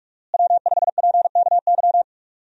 Morse code, heard as a keyed tone, about 42 wpm, 700 Hz. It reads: W5PKY
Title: USA Call Signs - 5 Characters - Repeated 3x - 45wpm